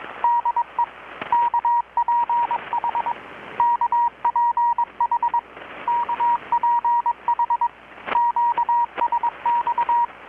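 Morse code, heard as a keyed tone, about 21 wpm, 950 Hz, with radio noise. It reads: DE KPH KPH KPH QSX
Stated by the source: Morse from KPH's Night of Nights XX maritime radiotelegraphy event, received in Chiapas, Mexico about 3,800 km away on a Kenwood R-600 with a grounded 10 m long-wire antenna.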